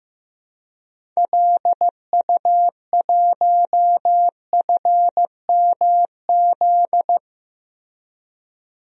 Morse code, heard as a keyed tone, about 15 wpm, 700 Hz. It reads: LU1FMZ
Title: DXCC-100 Call Signs - 6 Characters - Repeated 3x - 15wpm